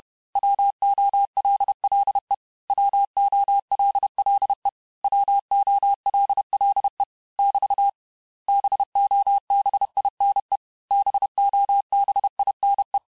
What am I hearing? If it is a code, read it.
Morse code, heard as a keyed tone, about 31 wpm, 800 Hz. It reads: WOLLE WOLLE WOLLE <BT> BOBINE BOBINE
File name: cours-du-mercredi / audios / cours-861-texte-les-mots-vitesse-29MPM